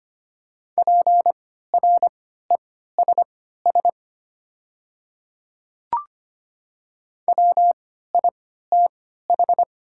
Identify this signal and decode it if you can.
Morse code, heard as a keyed tone, about 25 wpm, spaced as for 14 wpm, 690 Hz. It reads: PRESS WITH
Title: 3x Character Spacing - Top 500 Words - 25wpm